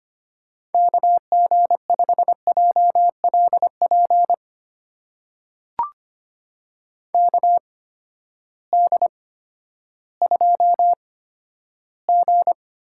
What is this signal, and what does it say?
Morse code, heard as a keyed tone, about 25 wpm, spaced as for 12 wpm, 700 Hz. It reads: KG5JLP KD2G
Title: USA Call Signs - 6 Characters - 8x Character Spacing - 25wpm